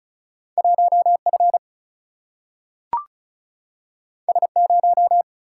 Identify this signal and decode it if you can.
Morse code, heard as a keyed tone, about 36 wpm, 700 Hz. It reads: 1F S0
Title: Single-Letter Single-Number 35wpm